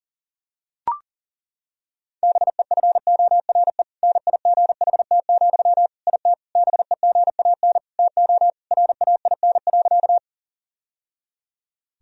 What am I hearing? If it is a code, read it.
Morse code, heard as a keyed tone, about 40 wpm, 700 Hz. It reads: BEFORE NIGHT, IT BEGAN TO RAIN.
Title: Sentences from Top 1000 Words - Review Part 1 - 40wpm